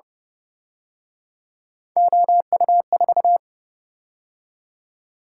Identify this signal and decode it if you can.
Morse code, heard as a keyed tone, about 30 wpm, 700 Hz. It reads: OU4